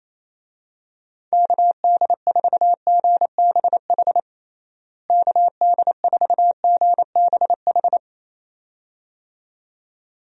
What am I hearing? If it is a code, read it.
Morse code, heard as a keyed tone, about 28 wpm, 700 Hz. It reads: KD4GBH KD4GBH
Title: USA Call Signs - 6 Characters - Repeated 2x - 28wpm